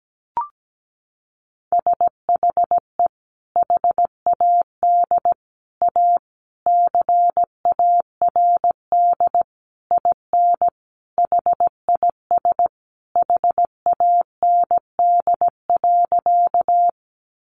Morse code, heard as a keyed tone, about 17 wpm, 700 Hz. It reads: SHE HAD A CARD IN HIS HAND.
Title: Sentences from Top 600 Words 17wpm